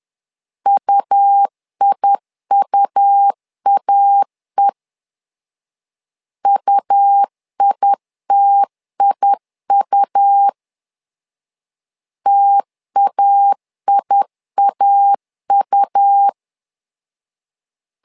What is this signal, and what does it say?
Morse code, heard as a keyed tone, about 10 wpm, 800 Hz. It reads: UIUAE UITIU TAIAU
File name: LEZIONI / 2 UHDN / 2 U 10wpm 1622